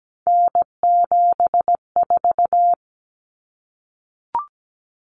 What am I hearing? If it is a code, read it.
Morse code, heard as a keyed tone, about 17 wpm, 700 Hz. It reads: N74
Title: Common Cut Numbers - 3-Digit 17wpm